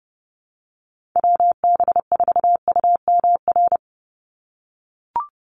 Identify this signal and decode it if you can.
Morse code, heard as a keyed tone, about 30 wpm, 700 Hz. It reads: WB4UMR
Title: USA Call Signs 30wpm